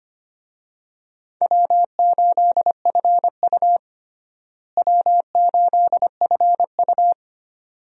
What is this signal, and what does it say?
Morse code, heard as a keyed tone, about 25 wpm, 700 Hz. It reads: W8FU W8FU
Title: USA Call Signs - 4 Characters - Repeated 2x - 25wpm